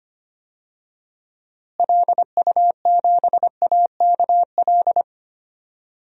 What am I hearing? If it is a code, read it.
Morse code, heard as a keyed tone, about 25 wpm, 700 Hz. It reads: LU7AKL
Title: DXCC-100 Call Signs - 6 Characters - Repeated 3x - 25wpm